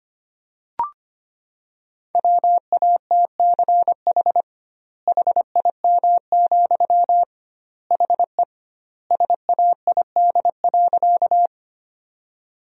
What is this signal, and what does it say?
Morse code, heard as a keyed tone, about 25 wpm, 700 Hz. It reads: WATCH HIM, HE SAID.
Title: 2x Word-Spacing - Sentences from Top 300 Words - 25wpm